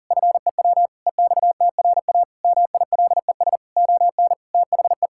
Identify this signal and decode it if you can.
Morse code, heard as a keyed tone, about 40 wpm, 700 Hz. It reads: FEW EXTRA MILES ON THE
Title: Pangram Challenge v2 - 40wpm